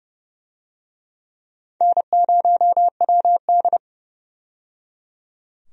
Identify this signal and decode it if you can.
Morse code, heard as a keyed tone, about 30 wpm, 700 Hz. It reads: N0WD
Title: USA Call Signs - 4 Characters - 30wpm